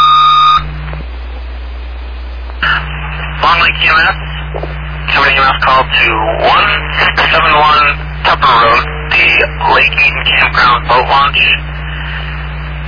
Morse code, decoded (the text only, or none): none